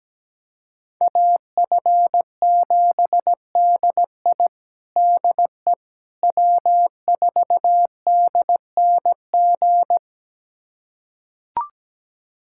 Morse code, heard as a keyed tone, about 17 wpm, 700 Hz. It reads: AF7DI DE W4DNG
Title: USA Callsign DE Callsign - 5 Characters - 17wpm